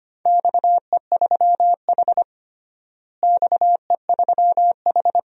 Morse code, heard as a keyed tone, about 25 wpm, 700 Hz. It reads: XE3H XE3H
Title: DXCC-100 Call Signs - 4 Characters - Repeated 2x - 25wpm